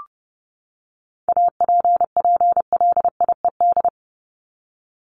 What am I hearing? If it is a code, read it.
Morse code, heard as a keyed tone, about 30 wpm, 700 Hz. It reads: APPLIED
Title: ED Suffix - Top 100 Words with Base Words 30wpm